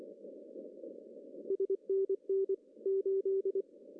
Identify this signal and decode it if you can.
Morse code, heard as a keyed tone, about 25 wpm, 390 Hz. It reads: SNN 8
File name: CW / cqww-cw-k5tr